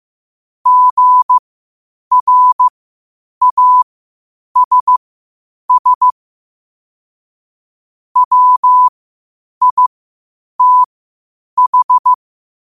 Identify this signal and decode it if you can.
Morse code, heard as a keyed tone, about 15 wpm, spaced as for 8 wpm, 1000 Hz. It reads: GRASS WITH
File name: qotdfiles / 8wpm